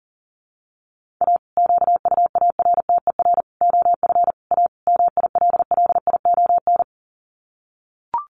Morse code, heard as a keyed tone, about 38 wpm, 700 Hz. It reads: A QUARTER OF A MILLION
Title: Sets of 5 Words Encore2 40wpm